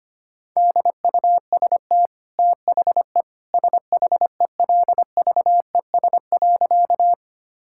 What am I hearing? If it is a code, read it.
Morse code, heard as a keyed tone, about 25 wpm, 690 Hz. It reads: DUST THE SHELVES.
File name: Easy Sentences with Hints - 25wpm